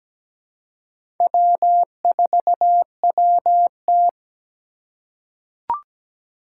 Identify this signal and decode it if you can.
Morse code, heard as a keyed tone, about 17 wpm, 700 Hz. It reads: W4WT
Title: USA Call Signs - 4 Characters - 4x Character Spacing - 17wpm